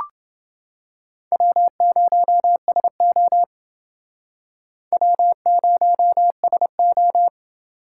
Morse code, heard as a keyed tone, about 28 wpm, 700 Hz. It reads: W0SO W0SO